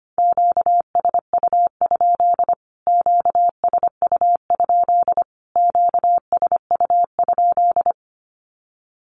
Morse code, heard as a keyed tone, about 25 wpm, 700 Hz. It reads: QSU? QSU? QSU?
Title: Q-Codes Repeated 3x 25wpm